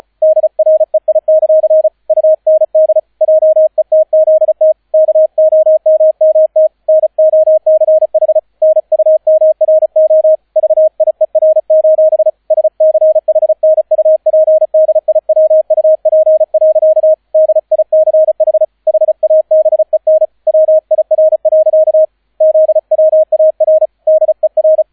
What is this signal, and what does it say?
Morse code, heard as a keyed tone, about 34 wpm, 630 Hz, with radio noise. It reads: DREI; UND JETZT KOMMT NOCH NUMRO VIER: SCHNUPDIWUP. DICH HABEN WIR. ZWAR DER